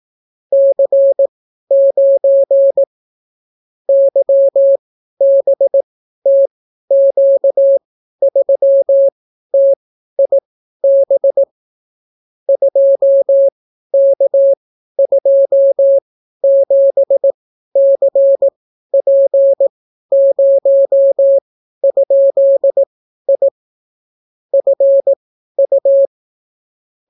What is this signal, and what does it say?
Morse code, heard as a keyed tone, about 18 wpm, spaced as for 12 wpm, 550 Hz. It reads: C9 YBTQ3TIB 2K27CP0?I FU